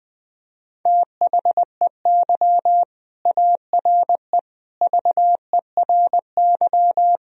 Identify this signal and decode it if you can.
Morse code, heard as a keyed tone, about 21 wpm, 700 Hz. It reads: THEY ARE VERY
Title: Sentences from Top 1000 Words - Review Part 2 - 20wpm